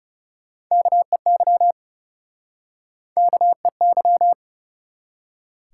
Morse code, heard as a keyed tone, about 40 wpm, 700 Hz. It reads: KEY KEY